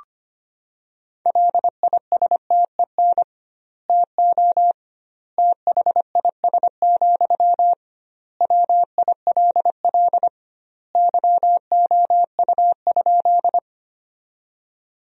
Morse code, heard as a keyed tone, about 25 wpm, 700 Hz. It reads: LISTEN TO THIS, WILL YOU?
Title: 2x Word-Spacing - Sentences from Top 400 Words - 25wpm